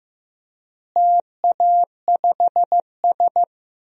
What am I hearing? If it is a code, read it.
Morse code, heard as a keyed tone, about 15 wpm, 700 Hz. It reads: TA5S
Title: DXCC-100 Call Signs - 4 Characters - Repeated 2x - 15wpm